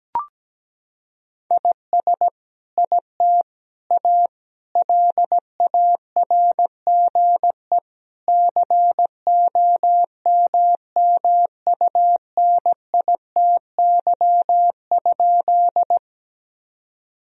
Morse code, heard as a keyed tone, about 17 wpm, 700 Hz. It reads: IS IT A LARGE COMMUNITY?
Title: Sentences from Top 1000 Words - Review Part 4 - 17wpm